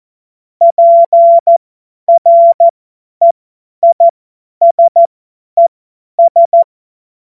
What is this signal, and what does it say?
Morse code, heard as a keyed tone, about 14 wpm, spaced as for 10 wpm, 680 Hz. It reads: PREISES
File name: Morsekurs / Witze / Witz 08. August 2023 50 F70